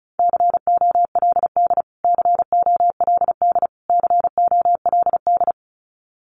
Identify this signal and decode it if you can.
Morse code, heard as a keyed tone, about 34 wpm, 700 Hz. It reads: COLD COLD COLD
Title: Top 500 Words Repeated 35wpm